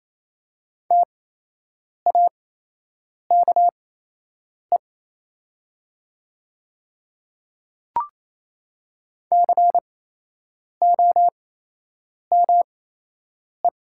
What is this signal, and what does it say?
Morse code, heard as a keyed tone, about 28 wpm, spaced as for 8 wpm, 700 Hz. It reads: TAKE COME
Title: 8x Character Spacing - Top 200 Words - 28wpm